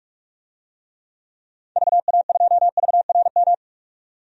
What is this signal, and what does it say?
Morse code, heard as a keyed tone, about 46 wpm, 700 Hz. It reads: VA2VRM
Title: DXCC-100 Call Signs - 6 Characters - 45wpm